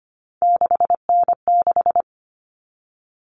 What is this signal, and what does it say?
Morse code, heard as a keyed tone, about 25 wpm, 700 Hz. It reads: 6N6
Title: All Cut Numbers - 3-Digit 25wpm